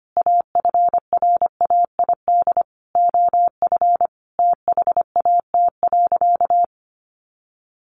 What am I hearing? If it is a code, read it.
Morse code, heard as a keyed tone, about 25 wpm, 700 Hz. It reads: AFRAID OF THAT.
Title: Sentences from Top 1500 Words 25wpm